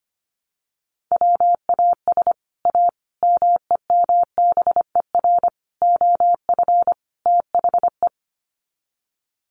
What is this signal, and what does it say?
Morse code, heard as a keyed tone, about 25 wpm, 700 Hz. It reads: WAS A MEMBER OF THE
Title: Sets of 5 Words 25wpm